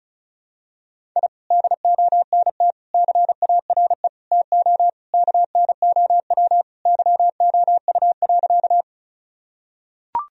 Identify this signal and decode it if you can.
Morse code, heard as a keyed tone, about 36 wpm, 700 Hz. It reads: I DONT CARE TO KNOW YOU.
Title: Sentences from Top 1000 Words - Review Part 1 - 35wpm